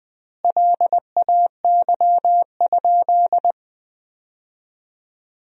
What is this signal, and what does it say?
Morse code, heard as a keyed tone, about 20 wpm, 700 Hz. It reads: LAY?
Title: 3x Word-Spacing - Sentences from Top 400 Words - 20wpm